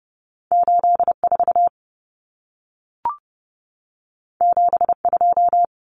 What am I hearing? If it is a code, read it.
Morse code, heard as a keyed tone, about 30 wpm, 700 Hz. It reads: 84 72
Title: Common Cut Numbers - 2-Digit 30wpm